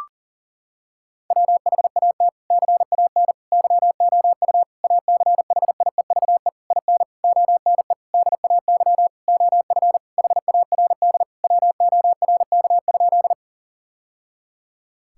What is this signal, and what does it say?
Morse code, heard as a keyed tone, about 37 wpm, 700 Hz. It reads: WHAT CAN YOU ACHIEVE IN ONE DAY OF HARD WORK?
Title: Sentences from Top 1000 Words - Review Part 6 - 40wpm